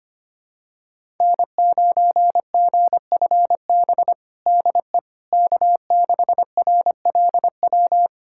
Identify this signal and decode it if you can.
Morse code, heard as a keyed tone, about 25 wpm, 700 Hz. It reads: N9GFB DE K6RLW
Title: USA Callsign DE Callsign - 5 Characters - 25wpm